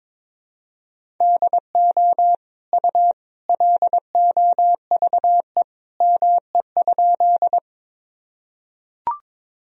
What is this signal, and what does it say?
Morse code, heard as a keyed tone, about 21 wpm, 700 Hz. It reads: DO U LOVE ME?
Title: Sentences from Top 400 Words and Abbreviations 22wpm